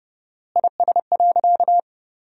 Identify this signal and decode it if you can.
Morse code, heard as a keyed tone, about 32 wpm, 700 Hz. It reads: IS.
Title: 4x Word-Spacing - Sentences from Top 500 Words - 30wpm